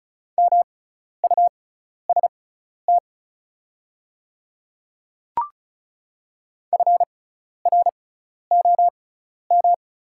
Morse code, heard as a keyed tone, about 34 wpm, spaced as for 12 wpm, 700 Hz. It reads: MUST FROM
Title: 6x Character Spacing - Top 200 Words - 35wpm